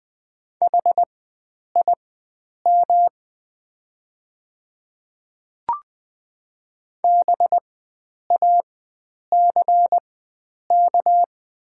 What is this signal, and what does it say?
Morse code, heard as a keyed tone, about 20 wpm, spaced as for 9 wpm, 700 Hz. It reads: HIM BACK
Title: 4x Character Spacing - Top 300 Words - 20wpm